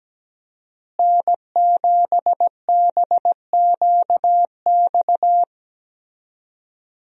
Morse code, heard as a keyed tone, about 17 wpm, 700 Hz. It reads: N7BQX
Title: USA Call Signs - 5 Characters - Repeated 3x - 17wpm